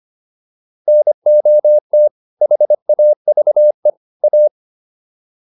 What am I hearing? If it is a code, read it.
Morse code, heard as a keyed tone, about 25 wpm, 600 Hz. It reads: NOT HAVE A